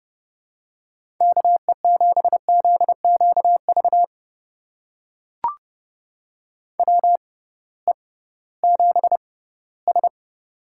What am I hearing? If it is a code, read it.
Morse code, heard as a keyed tone, about 31 wpm, spaced as for 18 wpm, 700 Hz. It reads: KE7ZQV WE7S
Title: USA Call Signs - 6x Character Spacing - 30wpm